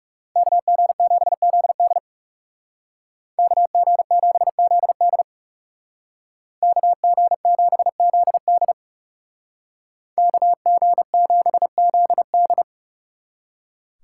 Morse code, heard as a keyed tone, about 39 wpm, 700 Hz. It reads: KG7ZD KG7ZD KG7ZD KG7ZD